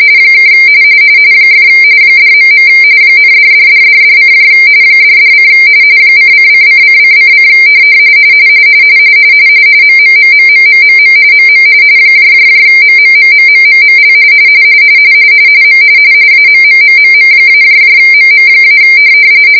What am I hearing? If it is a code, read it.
RTTY, about 45.45 baud. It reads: ( ! $.. ,$$$$$: $$: :::::::: MMMMMMMM MM MMMMMMML M AMMMMM-MM-